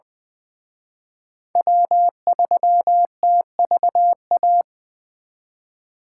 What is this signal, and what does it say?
Morse code, heard as a keyed tone, about 24 wpm, 700 Hz. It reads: W3TVA
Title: Speed Racing - USA 1x3 Call Signs 20wpm